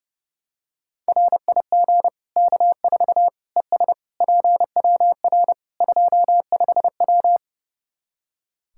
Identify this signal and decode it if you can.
Morse code, heard as a keyed tone, about 30 wpm, 700 Hz. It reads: RIG K4 ES PWR 25W
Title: Practice QSO - Exchange 2 - Info - 30wpm